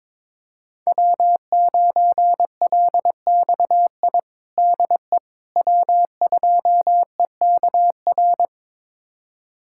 Morse code, heard as a keyed tone, about 22 wpm, 700 Hz. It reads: W9LXI DE W2EKR
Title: USA Callsign DE Callsign - 5 Characters - 22wpm